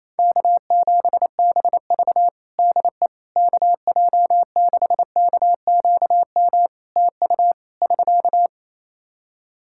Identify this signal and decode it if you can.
Morse code, heard as a keyed tone, about 28 wpm, 700 Hz. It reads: K7BV DE KJ6KQM TU <SK>
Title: Practice QSO - Exchange 4 - Ending - 28wpm